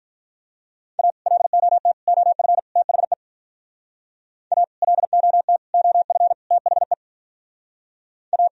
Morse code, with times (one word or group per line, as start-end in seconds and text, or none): 1.0-1.1 A
1.3-1.9 LOT
2.1-2.6 OF
2.8-3.1 THE
4.5-4.6 A
4.8-5.6 LOT
5.7-6.3 OF
6.5-6.9 THE
8.3-8.5 A